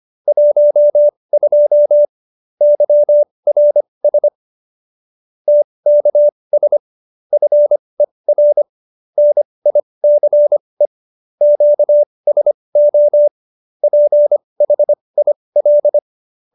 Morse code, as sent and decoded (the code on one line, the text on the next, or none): .---- ..--- / -.-- .-. ... / - -.- ... / ..-. .-. / -. .. -.-. / --.- ... --- / .--. .... .. .-..
12 YRS TKS FER NICE QSO PHIL